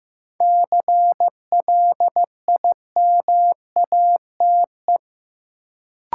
Morse code, heard as a keyed tone, about 15 wpm, 700 Hz. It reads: CLIMATE
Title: Sentences from Top 1600 Words - New Word Drill - 15wpm